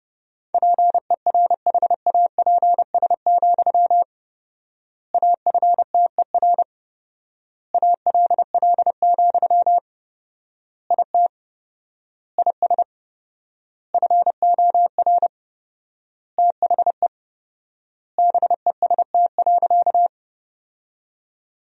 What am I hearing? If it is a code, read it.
Morse code, heard as a keyed tone, about 30 wpm, 700 Hz. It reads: PERHAPS, AFTER ALL, IT IS FOR THE BEST.
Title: 4x Word-Spacing - Sentences from Top 500 Words - 30wpm